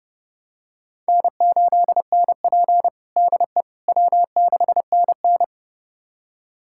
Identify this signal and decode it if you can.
Morse code, heard as a keyed tone, about 30 wpm, 700 Hz. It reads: N8NP DE W6NN
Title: USA Callsign DE Callsign - 4 Characters - 30wpm